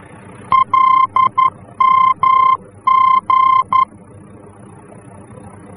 Morse code, heard as a keyed tone, about 11 wpm, 1050 Hz. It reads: LMG
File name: france / mx / LMG-F-VOR